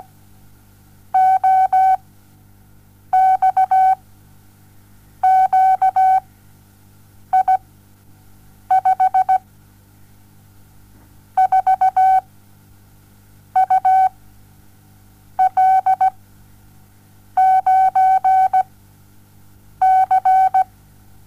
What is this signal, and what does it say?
Morse code, heard as a keyed tone, about 17 wpm, 760 Hz. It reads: O X Q I 5 4 U L 9 C
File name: Onlinelehrgang / morsen / DJ4UF Morsen07